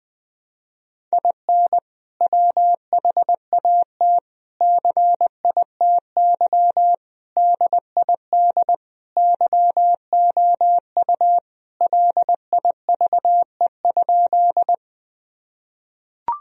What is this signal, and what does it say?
Morse code, heard as a keyed tone, about 20 wpm, 700 Hz. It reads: IN WHAT CITY DID YOU LIVE?
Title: Sentences from Top 1000 Words - Review Part 1 - 20wpm